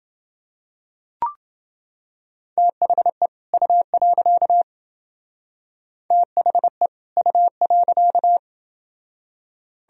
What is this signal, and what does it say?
Morse code, heard as a keyed tone, about 28 wpm, 700 Hz. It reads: THE U. THE U.